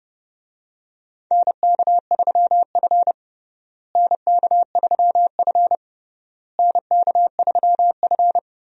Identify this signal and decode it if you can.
Morse code, heard as a keyed tone, about 30 wpm, 700 Hz. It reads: NK3F NK3F NK3F
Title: USA Call Signs - 4 Characters - Repeated 3x - 30wpm